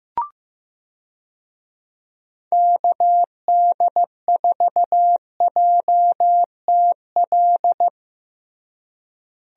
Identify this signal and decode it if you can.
Morse code, heard as a keyed tone, about 15 wpm, 700 Hz. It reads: KD4JTL
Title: USA Call Signs - 6 Characters - Repeated 2x - 15wpm